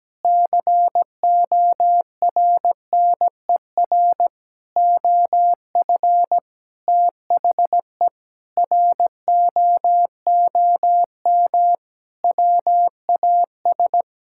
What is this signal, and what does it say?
Morse code, heard as a keyed tone, about 17 wpm, 700 Hz. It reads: CORNER OF THE ROOM WAS
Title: Sentences from Top 1600 Words 17wpm